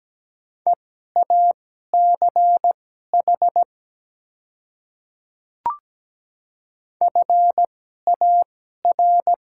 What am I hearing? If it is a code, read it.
Morse code, heard as a keyed tone, about 17 wpm, spaced as for 12 wpm, 690 Hz. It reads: EACH FAR
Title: 2x Character Spacing - Top 300 Words - 17wpm